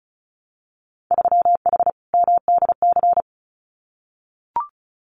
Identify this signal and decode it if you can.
Morse code, heard as a keyed tone, about 34 wpm, 700 Hz. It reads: 3H MDC
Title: Winter Field Day Exchanges Encore 35wpm